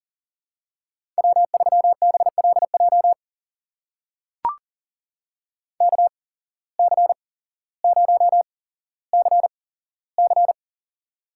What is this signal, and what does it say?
Morse code, heard as a keyed tone, about 38 wpm, spaced as for 17 wpm, 700 Hz. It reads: W3BLJ KC0CC
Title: USA Call Signs - 5 Characters - 8x Character Spacing - 40wpm